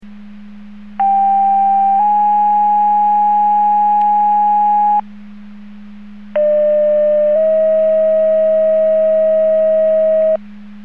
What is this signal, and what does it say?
Two-tone page, heard as a one-second tone, then a three-second tone, about 790 then 830 Hz, about 600 then 640 Hz. It is heard from 1.0 to 10.4 s.